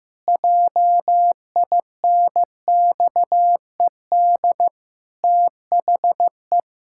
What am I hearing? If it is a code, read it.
Morse code, heard as a keyed tone, about 15 wpm, 700 Hz. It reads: JINXED THE
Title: Pangram Challenge v2 - 15wpm